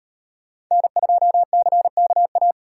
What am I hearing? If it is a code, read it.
Morse code, heard as a keyed tone, about 38 wpm, 700 Hz. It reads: N2CKA